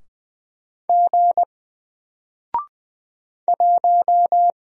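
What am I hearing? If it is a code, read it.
Morse code, heard as a keyed tone, about 20 wpm, 700 Hz. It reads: G 1
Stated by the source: Single letter and number Morse drill at 20 wpm, from N4LCD.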